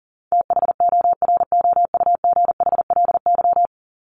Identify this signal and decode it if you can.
Morse code, heard as a keyed tone, about 40 wpm, 700 Hz. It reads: THOROUGHLY
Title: LY Suffix - Top 100 Words with Base Words 40wpm